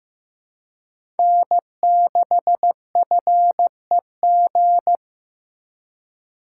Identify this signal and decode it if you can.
Morse code, heard as a keyed tone, about 15 wpm, 700 Hz. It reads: N6FEG
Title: USA Call Signs - 4x Character Spacing - 15wpm